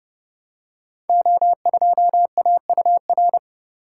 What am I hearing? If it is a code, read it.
Morse code, heard as a keyed tone, about 30 wpm, 700 Hz. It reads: O2AUR
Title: Code Groups - 5 Letter:Numbers - 30wpm